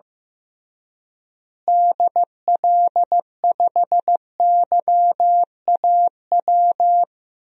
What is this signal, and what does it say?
Morse code, heard as a keyed tone, about 15 wpm, 700 Hz. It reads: DL5YAW
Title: DXCC-100 Call Signs - 6 Characters - Repeated 3x - 15wpm